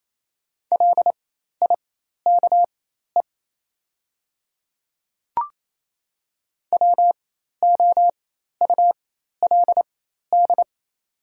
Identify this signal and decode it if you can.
Morse code, heard as a keyed tone, about 28 wpm, spaced as for 13 wpm, 690 Hz. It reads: LIKE WOULD
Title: 4x Character Spacing - Top 100 Words - 28wpm